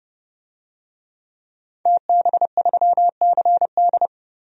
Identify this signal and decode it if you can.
Morse code, heard as a keyed tone, about 30 wpm, 700 Hz. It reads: TB3CD
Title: DXCC-100 Call Signs - 5 Characters - 30wpm